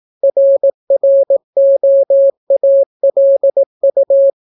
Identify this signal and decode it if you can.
Morse code, heard as a keyed tone, about 18 wpm, 550 Hz. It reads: RROALU